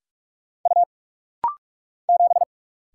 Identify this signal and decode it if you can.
Morse code, heard as a keyed tone, about 44 wpm, 700 Hz. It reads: U 7
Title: Single Letter-Number Alt - Rapid-Fire 45wpm